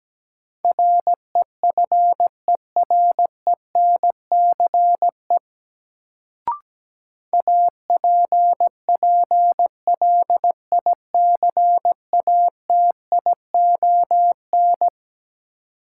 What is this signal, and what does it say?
Morse code, heard as a keyed tone, about 17 wpm, 700 Hz. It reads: REFERENCE APPLICATION
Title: Sentences from Top 1200 Words - New Word Drill - 17wpm